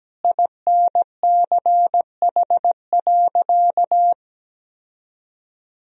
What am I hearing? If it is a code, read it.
Morse code, heard as a keyed tone, about 17 wpm, 700 Hz. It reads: INCH.
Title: Sentences from Top 1000 Words - Review Part 3 - 17wpm